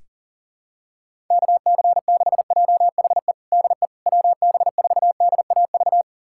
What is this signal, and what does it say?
Morse code, heard as a keyed tone, about 40 wpm, 700 Hz. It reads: KC6JHE DE WB4DAV
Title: USA Callsign DE Callsign 40wpm